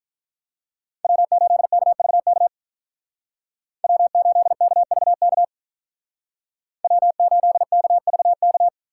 Morse code, heard as a keyed tone, about 43 wpm, 700 Hz. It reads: W8KVK W8KVK W8KVK
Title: Speed Racing - USA 1x3 Call Signs 35wpm